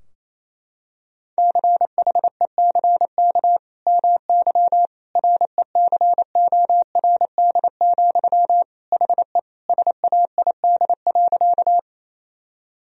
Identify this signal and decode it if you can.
Morse code, heard as a keyed tone, about 28 wpm, 700 Hz. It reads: CHECK MY RECORD, HE SAID.
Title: Sentences from Top 1000 Words - Review Part 3 - 28wpm